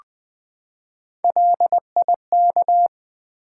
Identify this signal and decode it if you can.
Morse code, heard as a keyed tone, about 20 wpm, 700 Hz. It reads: LIK
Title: Build A Word - 3 Letters At A Time - 6-Letter Words - 20wpm